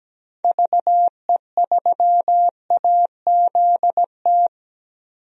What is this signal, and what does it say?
Morse code, heard as a keyed tone, about 17 wpm, 700 Hz. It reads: VE3AZT
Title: DXCC-100 Call Signs - 6 Characters - 17wpm